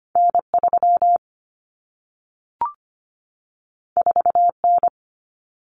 Morse code, heard as a keyed tone, about 25 wpm, 700 Hz. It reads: N3 4N
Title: Common Cut Numbers - 2-Digit 25wpm